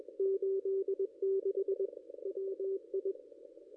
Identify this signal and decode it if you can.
Morse code, heard as a keyed tone, about 20 wpm, 380 Hz, with radio noise. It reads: 86 WI
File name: contests / ss-cw-k5tr-op-n5rz -0400